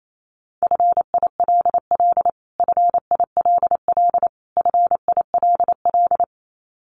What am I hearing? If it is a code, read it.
Morse code, heard as a keyed tone, about 28 wpm, 700 Hz. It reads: FILL FILL FILL
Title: Top 500 Words Repeated 28wpm